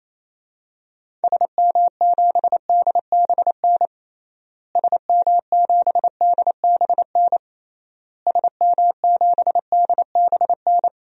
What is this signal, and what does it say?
Morse code, heard as a keyed tone, about 27 wpm, 700 Hz. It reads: SM7DBN SM7DBN SM7DBN
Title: DXCC-100 Call Signs - 6 Characters - Repeated 3x - 28wpm